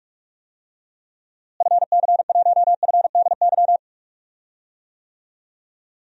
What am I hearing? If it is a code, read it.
Morse code, heard as a keyed tone, about 45 wpm, 700 Hz. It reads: FC1FDY
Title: DXCC-100 Call Signs - 6 Characters - Repeated 3x - 45wpm